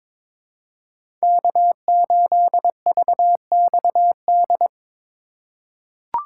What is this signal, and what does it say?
Morse code, heard as a keyed tone, about 22 wpm, 700 Hz. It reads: K8VXD
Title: USA Call Signs - 5 Characters - 3x Character Spacing - 22wpm